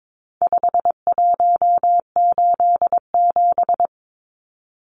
Morse code, heard as a keyed tone, about 22 wpm, 700 Hz. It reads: 5187
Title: Numbers - 4-Digit 22wpm